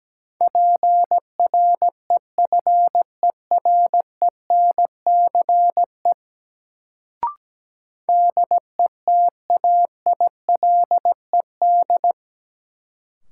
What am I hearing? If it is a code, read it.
Morse code, heard as a keyed tone, about 17 wpm, 700 Hz. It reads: PREFERENCE DETAILED
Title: Sentences from Top 1500 Words - New Word Drill Alt - 17wpm